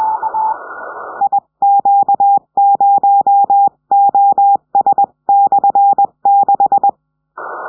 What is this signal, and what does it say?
Morse code, heard as a keyed tone, about 21 wpm, 800 Hz. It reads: IQ0OS/6
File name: 70cm